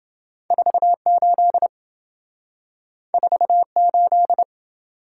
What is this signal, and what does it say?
Morse code, heard as a keyed tone, about 27 wpm, 700 Hz. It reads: 48 48